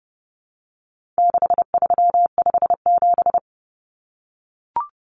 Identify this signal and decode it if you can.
Morse code, heard as a keyed tone, about 30 wpm, 700 Hz. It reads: 6357
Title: Common Cut Numbers - 4-Digit 30wpm